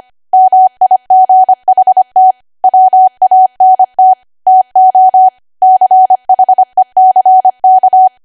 Morse code, heard as a keyed tone, about 25 wpm, 750 Hz. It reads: MIGHT WANT TO CHECK
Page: 25 WPM Code Archive